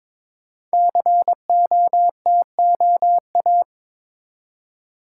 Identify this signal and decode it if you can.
Morse code, heard as a keyed tone, about 22 wpm, 700 Hz. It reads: COTOA
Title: Code Groups - 5 Letters - 22wpm